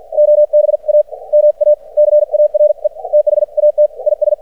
Morse code, heard as a keyed tone, about 56 wpm, spaced as for 40 wpm, 600 Hz. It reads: OBAMACARE THAT IS
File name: archived / CRSnet